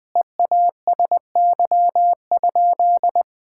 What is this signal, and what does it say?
Morse code, heard as a keyed tone, about 21 wpm, 700 Hz. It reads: EASY?
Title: Sentences from Top 1000 Words - Review Part 5 - 20wpm